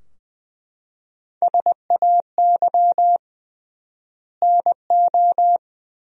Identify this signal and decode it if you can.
Morse code, heard as a keyed tone, about 20 wpm, 700 Hz. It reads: SAY NO
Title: 3x Word-Spacing - Sentences from Top 200 Words - 20wpm